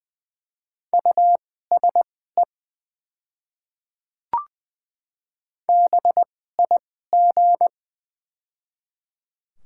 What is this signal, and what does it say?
Morse code, heard as a keyed tone, about 20 wpm, spaced as for 14 wpm, 700 Hz. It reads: USE BIG
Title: 2x Character Spacing - Top 200 Words - 20wpm